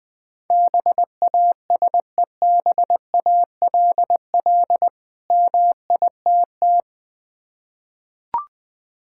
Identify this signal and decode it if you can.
Morse code, heard as a keyed tone, about 20 wpm, 700 Hz. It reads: BASEBALL MITT
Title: Difficult Words with Hints - 20wpm